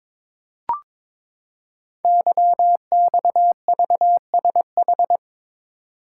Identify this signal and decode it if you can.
Morse code, heard as a keyed tone, about 22 wpm, 700 Hz. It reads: YXVSH
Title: Code Groups - 5 Letter:Numbers - 22wpm